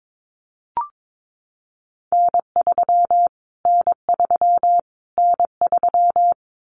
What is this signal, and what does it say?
Morse code, heard as a keyed tone, about 22 wpm, 700 Hz. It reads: N3 N3 N3
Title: Callsign Prefixes for Top 25 DXCC - Repeated 3x - 22wpm